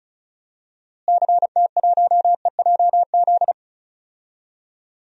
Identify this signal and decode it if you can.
Morse code, heard as a keyed tone, about 36 wpm, 700 Hz. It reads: CT1EJZ
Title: DXCC-100 Call Signs - 6 Characters - Repeated 2x - 35wpm